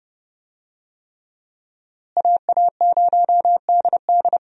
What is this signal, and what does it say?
Morse code, heard as a keyed tone, about 30 wpm, 700 Hz. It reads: AA0DD
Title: USA Call Signs - 5 Characters - Repeated 3x - 30wpm